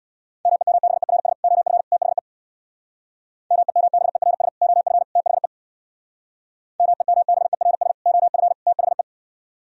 Morse code, heard as a keyed tone, about 64 wpm, 700 Hz. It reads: MEMBERS OF THE MEMBERS OF THE MEMBERS OF THE